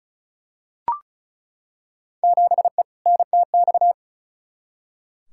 Morse code, heard as a keyed tone, about 35 wpm, 700 Hz. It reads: 7E NTX
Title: Field Day Exchanges Encore - 35wpm